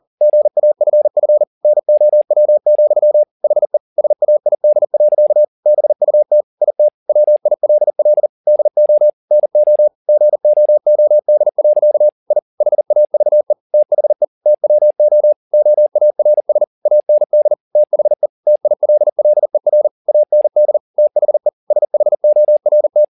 Morse code, heard as a keyed tone, about 40 wpm, 600 Hz. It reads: GAFF NOW, HE SAID. BUT IT WILL DO NO GOOD. I HAVE THE TWO OARS AND THE TILLER AND THE SHORT
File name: the old man and the sea 2 40wpm